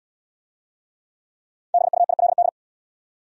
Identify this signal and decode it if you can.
Morse code, heard as a keyed tone, about 67 wpm, 700 Hz. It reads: BUILD